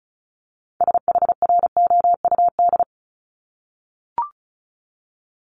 Morse code, heard as a keyed tone, about 36 wpm, 700 Hz. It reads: SHROUD
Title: Multi-letter Phonemes 35wpm